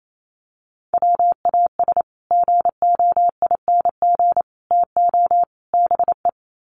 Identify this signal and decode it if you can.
Morse code, heard as a keyed tone, about 28 wpm, 700 Hz. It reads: WAS GOING TO BE
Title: Sets of 4 Words 28wpm